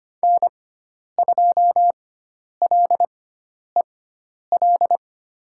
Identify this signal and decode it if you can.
Morse code, heard as a keyed tone, about 25 wpm, spaced as for 10 wpm, 700 Hz. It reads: N2LEL
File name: Lesson 16g - Letter c - Call Signs 25wpm